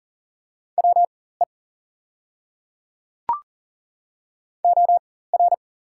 Morse code, heard as a keyed tone, about 39 wpm, spaced as for 19 wpm, 690 Hz. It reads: WE OR
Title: 4x Character Spacing - Top 100 Words - 40wpm